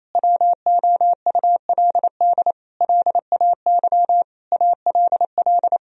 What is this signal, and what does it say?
Morse code, heard as a keyed tone, about 28 wpm, 700 Hz. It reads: WOULD LAY ALL
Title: Sentences from Top 1000 Words - Review Part 2 - 28wpm